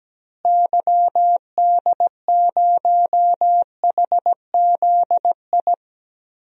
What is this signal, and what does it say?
Morse code, heard as a keyed tone, about 17 wpm, 700 Hz. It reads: YD0HZI
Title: DXCC-100 Call Signs - 6 Characters - Repeated 3x - 17wpm